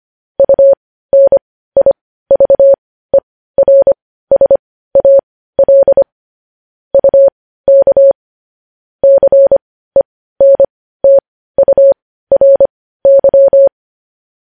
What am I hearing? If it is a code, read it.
Morse code, heard as a keyed tone, about 25 wpm, spaced as for 15 wpm, 550 Hz. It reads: UNIVERSAL UK CENTURY
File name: podcast / News headlines-1x-25x15